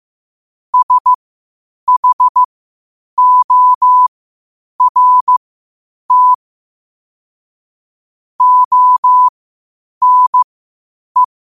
Morse code, heard as a keyed tone, about 15 wpm, spaced as for 8 wpm, 1000 Hz. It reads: SHORT ONE
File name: qotdfiles / 8wpm